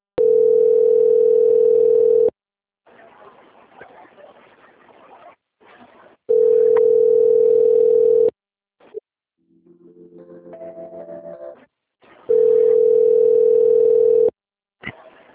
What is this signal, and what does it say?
Ringback tone (North America), 440 + 480 Hz.
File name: rg-1600-unknown- - - .43000